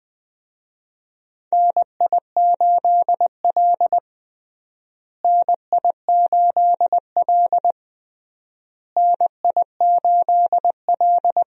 Morse code, heard as a keyed tone, about 20 wpm, 700 Hz. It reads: NI8L NI8L NI8L
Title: USA Call Signs - 4 Characters - Repeated 3x - 20wpm